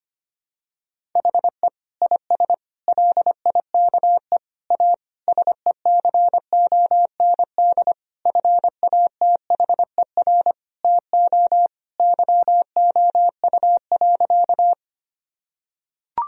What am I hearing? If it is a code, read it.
Morse code, heard as a keyed tone, about 25 wpm, 700 Hz. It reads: HE IS LIKE A SECOND FATHER TO YOU.
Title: Sentences from Top 1000 Words - Review Part 1 - 25wpm